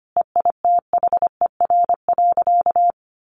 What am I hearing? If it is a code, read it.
Morse code, heard as a keyed tone, about 25 wpm, 700 Hz. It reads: EITHER.
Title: Sentences from Top 1000 Words 25wpm